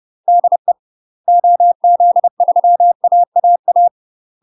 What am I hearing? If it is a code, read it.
Morse code, heard as a keyed tone, about 30 wpm, 700 Hz. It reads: DE OZ3AAA
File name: UK MASTER QSOs TRAINING / 30WPM / UK MASTER QSOs 0004